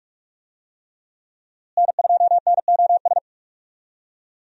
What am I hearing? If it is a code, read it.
Morse code, heard as a keyed tone, about 44 wpm, 700 Hz. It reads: N2NOS
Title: USA Call Signs - 5 Characters - Repeated 3x - 45wpm